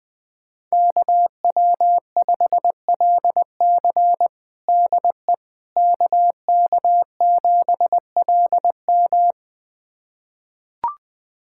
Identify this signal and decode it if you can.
Morse code, heard as a keyed tone, about 21 wpm, 700 Hz. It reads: KW5LC DE KK7LM
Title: USA Callsign DE Callsign - 5 Characters - 20wpm